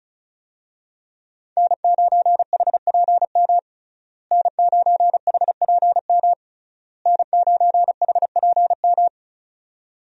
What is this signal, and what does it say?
Morse code, heard as a keyed tone, about 34 wpm, 700 Hz. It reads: N9HPM N9HPM N9HPM
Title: USA Call Signs - 5 Characters - Repeated 3x - 35wpm